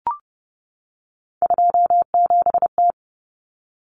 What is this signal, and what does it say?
Morse code, heard as a keyed tone, about 30 wpm, 700 Hz. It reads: 27T
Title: Common Cut Numbers - 3-Digit 30wpm